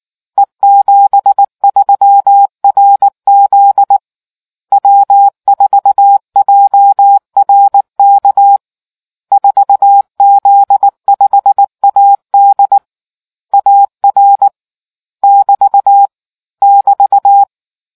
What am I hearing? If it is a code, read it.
Morse code, heard as a keyed tone, about 19 wpm, 800 Hz. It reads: E73RZ W4JRK 4Z5AD AR <BT> <BT>